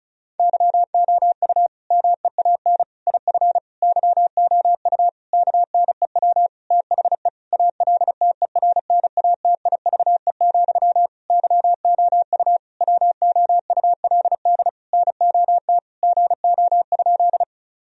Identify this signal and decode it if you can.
Morse code, heard as a keyed tone, about 36 wpm, 700 Hz. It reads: YOU MEAN IF YOU KNEW THE ALTERNATIVE, YOU WOULD NOT GO?
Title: Sentences from Top 1000 Words - Review Part 4 - 35wpm